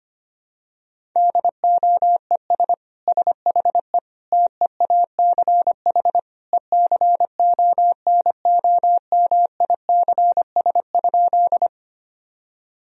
Morse code, heard as a keyed tone, about 25 wpm, 700 Hz. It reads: DOES SHE TEACH ECONOMICS?
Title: Sentences from Top 1000 Words - Review Part 3 - 25wpm